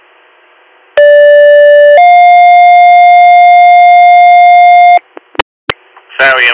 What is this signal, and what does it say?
Two-tone page, heard as a one-second tone, then a three-second tone, about 580 then 710 Hz, with speech. It is heard from 1.0 to 5.0 s.